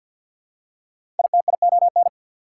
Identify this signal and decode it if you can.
Morse code, heard as a keyed tone, about 52 wpm, 690 Hz. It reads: ITION
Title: Prefixes and Suffixes 50wpm